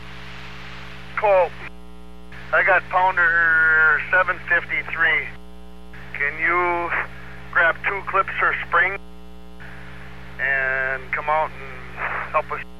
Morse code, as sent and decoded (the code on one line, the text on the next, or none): none